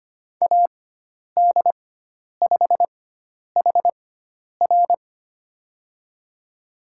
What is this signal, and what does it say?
Morse code, heard as a keyed tone, about 25 wpm, spaced as for 10 wpm, 700 Hz. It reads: AD5HR
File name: Lesson 16g - Letter c - Call Signs 25wpm